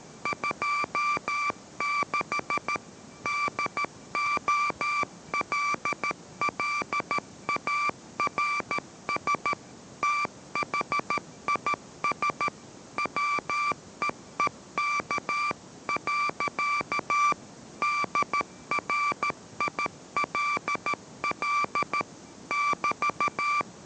Morse code, heard as a keyed tone, about 13 wpm, 1170 Hz, with noise. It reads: 26 DOLLARS THIS WEEK. DRILL <BT>